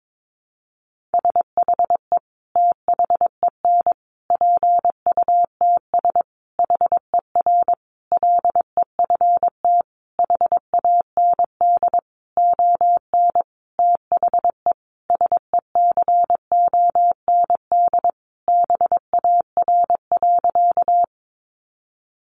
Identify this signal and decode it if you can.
Morse code, heard as a keyed tone, about 22 wpm, 700 Hz. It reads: SHE THEN PUTS HER LEFT HAND ON THE SECOND BAR.
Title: Easy Phrases 22wpm